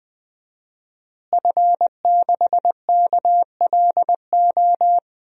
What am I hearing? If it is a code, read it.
Morse code, heard as a keyed tone, about 21 wpm, 700 Hz. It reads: F6KLO
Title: DXCC-100 Call Signs - 5 Characters - Repeated 3x - 20wpm